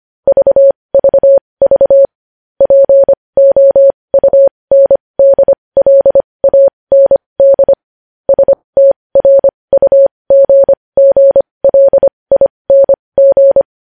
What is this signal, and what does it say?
Morse code, heard as a keyed tone, about 25 wpm, spaced as for 20 wpm, 550 Hz. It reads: VVV POUNDLAND STRUGGLING
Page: News headlines in Morse code at 20 wpm